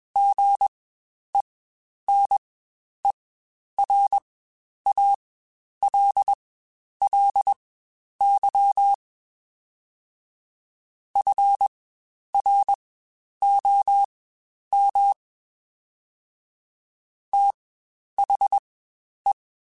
Morse code, heard as a keyed tone, about 21 wpm, spaced as for 10 wpm, 800 Hz. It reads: GENERALLY FROM THE
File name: Spaced Characters / Spaced Characters-mp3s / Weather5 5 sp @20